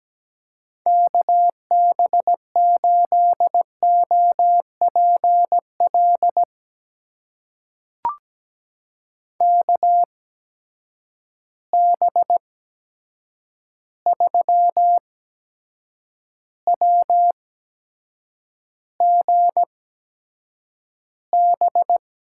Morse code, heard as a keyed tone, about 17 wpm, spaced as for 7 wpm, 700 Hz. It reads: KB8OPL KB3WGB
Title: USA Call Signs - 6 Characters - 8x Character Spacing - 17wpm